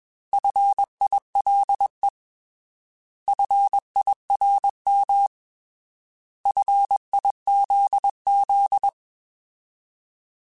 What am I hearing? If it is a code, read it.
Morse code, heard as a keyed tone, about 21 wpm, 800 Hz. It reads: FILE FIRM FIZZ
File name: Alphabetical / Alphabetical mp3s / Fs @20